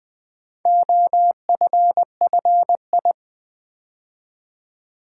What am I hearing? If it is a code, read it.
Morse code, heard as a keyed tone, about 20 wpm, 700 Hz. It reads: OFFI
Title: Build A Word - 4 Letters At A Time - 8-Letter Words - 20wpm